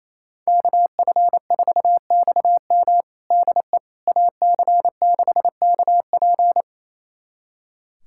Morse code, heard as a keyed tone, about 28 wpm, 700 Hz. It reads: KF4XM DE AC6KP
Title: USA Callsign DE Callsign - 5 Characters - 28wpm